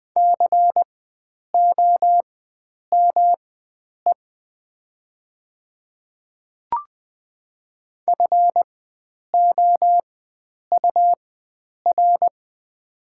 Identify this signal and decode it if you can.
Morse code, heard as a keyed tone, about 20 wpm, spaced as for 9 wpm, 700 Hz. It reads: COME FOUR
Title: 4x Character Spacing - Top 300 Words - 20wpm